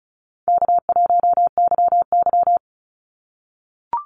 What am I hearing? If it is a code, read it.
Morse code, heard as a keyed tone, about 36 wpm, 700 Hz. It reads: K1YY
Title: Call Signs - 35wpm